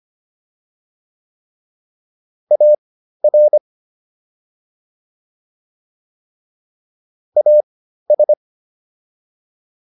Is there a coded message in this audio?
Yes: Morse code from 2.5 to 8.3 s, heard as a keyed tone, about 25 wpm, spaced as for 13 wpm, 600 Hz. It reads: AR AS